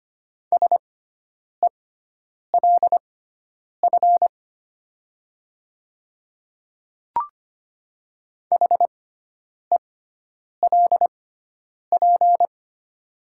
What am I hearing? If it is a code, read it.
Morse code, heard as a keyed tone, about 25 wpm, spaced as for 9 wpm, 700 Hz. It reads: SELF HELP
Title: 6x Character Spacing - Top 200 Words - 25wpm